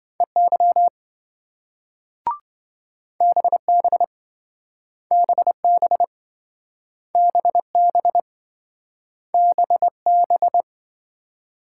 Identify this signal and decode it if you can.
Morse code, heard as a keyed tone, about 27 wpm, 700 Hz. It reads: EY BB BB BB BB